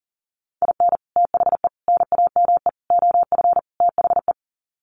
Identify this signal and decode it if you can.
Morse code, heard as a keyed tone, about 40 wpm, 700 Hz. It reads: IN THE NAME OF THE
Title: Sets of 5 Words Encore 40wpm